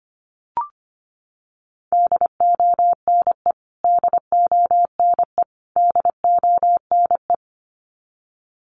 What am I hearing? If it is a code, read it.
Morse code, heard as a keyed tone, about 25 wpm, 700 Hz. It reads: DONE DONE DONE
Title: Top 500 Words Repeated 25wpm